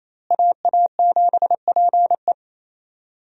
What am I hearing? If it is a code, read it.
Morse code, heard as a keyed tone, about 28 wpm, 700 Hz. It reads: AA7PE
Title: USA Call Signs - 5 Characters - Repeated 3x - 28wpm